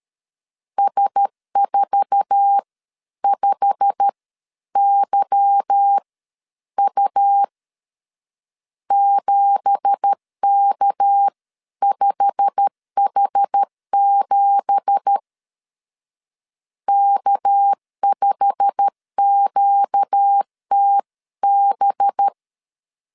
Morse code, heard as a keyed tone, about 13 wpm, spaced as for 11 wpm, 800 Hz. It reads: S45YU 7K5H7 K5QTB
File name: CW / DET / 1 9 1500C S C 12wpm 3811